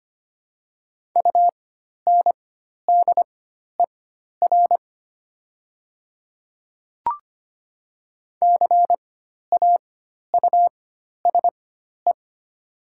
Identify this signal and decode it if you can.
Morse code, heard as a keyed tone, about 25 wpm, spaced as for 12 wpm, 700 Hz. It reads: UNDER CAUSE
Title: 4x Character Spacing - Top 200 Words - 25wpm